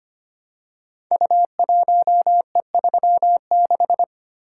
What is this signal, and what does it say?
Morse code, heard as a keyed tone, about 25 wpm, 700 Hz. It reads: U1E36
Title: Code Groups - 5 Letter:Numbers - 25wpm